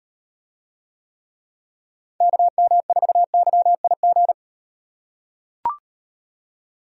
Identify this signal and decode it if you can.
Morse code, heard as a keyed tone, about 38 wpm, 700 Hz. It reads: KM4YIG